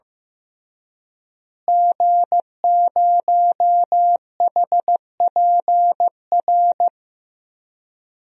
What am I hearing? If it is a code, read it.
Morse code, heard as a keyed tone, about 15 wpm, 700 Hz. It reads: G0HPR
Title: DXCC-100 Call Signs - 5 Characters - Repeated 3x - 15wpm